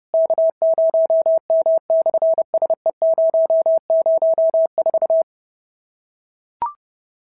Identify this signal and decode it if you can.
Morse code, heard as a keyed tone, about 30 wpm, 650 Hz. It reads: K0M/SE004
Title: SOTA QSO Practice 30wpm